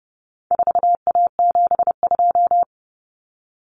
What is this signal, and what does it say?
Morse code, heard as a keyed tone, about 30 wpm, 700 Hz. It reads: 4A72
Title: Common Cut Numbers - 4-Digit 30wpm